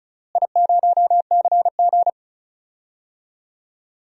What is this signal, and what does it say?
Morse code, heard as a keyed tone, about 36 wpm, 700 Hz. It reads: I0CG
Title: DXCC-100 Call Signs - 4 Characters - Repeated 2x - 35wpm